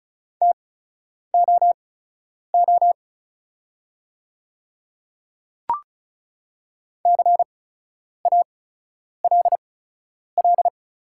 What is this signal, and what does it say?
Morse code, heard as a keyed tone, about 36 wpm, spaced as for 10 wpm, 700 Hz. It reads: TOO CALL
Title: 8x Character Spacing - Top 200 Words - 35wpm